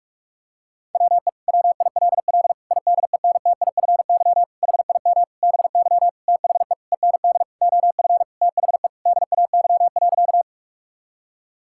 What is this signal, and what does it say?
Morse code, heard as a keyed tone, about 44 wpm, 700 Hz. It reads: WE WILL IDENTIFY HIM BY THE END OF THE DAY.
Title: Sentences from Top 1000 Words - Review Part 6 - 45wpm